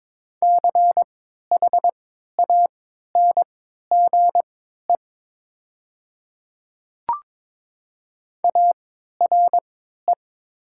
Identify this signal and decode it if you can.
Morse code, heard as a keyed tone, about 22 wpm, spaced as for 12 wpm, 700 Hz. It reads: CHANGE ARE
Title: 3x Character Spacing - Top 500 Words - 22wpm